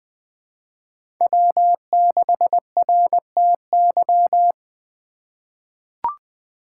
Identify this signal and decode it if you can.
Morse code, heard as a keyed tone, about 20 wpm, 700 Hz. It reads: W6RTY
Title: USA Call Signs - 5 Characters - 4x Character Spacing - 20wpm